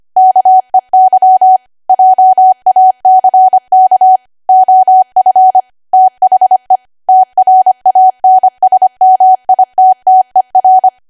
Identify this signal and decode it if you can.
Morse code, heard as a keyed tone, about 25 wpm, 750 Hz. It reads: KEY JACK OF THE TRANSMITTER
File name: Archive / 25 WPM / 25WPM